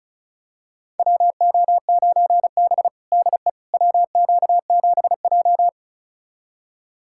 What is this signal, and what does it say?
Morse code, heard as a keyed tone, about 34 wpm, 700 Hz. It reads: WO9B DE WQ7J
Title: USA Callsign DE Callsign - 4 Characters - 35wpm